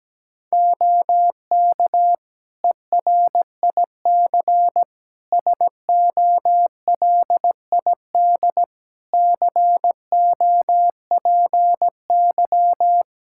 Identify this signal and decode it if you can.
Morse code, heard as a keyed tone, about 17 wpm, 700 Hz. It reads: OK ERIC SOLID COPY
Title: Practice QSO - Exchange 3 - More Info - 17wpm